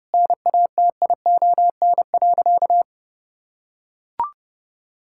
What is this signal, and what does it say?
Morse code, heard as a keyed tone, about 32 wpm, 700 Hz. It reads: NATION.
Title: Sentences from Top 1000 Words - Review Part 6 - 30wpm